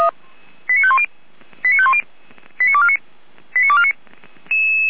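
Five-tone selective call (ZVEI), heard as quick runs of five tones, tones of about 70 ms. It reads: 89510 89510 89249 89249
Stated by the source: ZVEI five-tone selective call received via SDR, from the Feuerwehr Orlamünde audio server.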